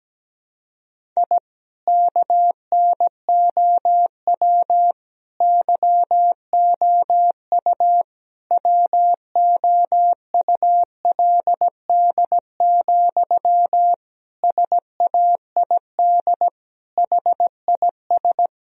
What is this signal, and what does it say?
Morse code, heard as a keyed tone, about 17 wpm, 700 Hz. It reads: I KNOW YOU WOULD, SAID HIS
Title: Sentences from Top 1000 Words - Review Part 1 - 17wpm